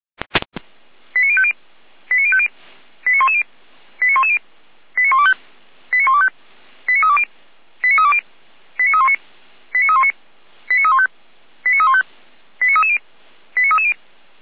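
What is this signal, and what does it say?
Five-tone selective call (ZVEI), heard as quick runs of five tones, tones of about 70 ms. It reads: 89050 89050 89110 89110 89125 89125 89320 89320 89319 89319 89315 89315 89330 89330
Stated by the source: ZVEI five-tone call 89097 received on SDR.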